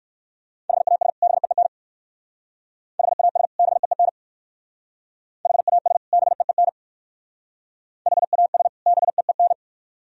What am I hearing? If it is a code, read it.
Morse code, heard as a keyed tone, about 55 wpm, 700 Hz. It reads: HAS BEEN HAS BEEN HAS BEEN HAS BEEN
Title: Speed-Racing - Sets of 2 Words 45wpm